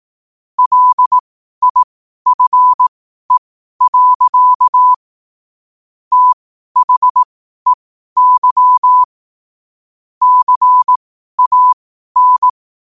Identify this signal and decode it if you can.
Morse code, heard as a keyed tone, about 18 wpm, spaced as for 12 wpm, 1000 Hz. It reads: LIFE. THEY CAN
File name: qotdfiles / 12wpm